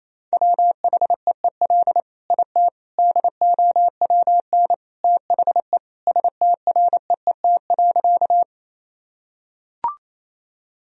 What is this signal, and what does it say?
Morse code, heard as a keyed tone, about 27 wpm, 700 Hz. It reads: WHEEL IT DOWN THE STREET.
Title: Sentences from Top 1000 Words - Review Part 3 - 28wpm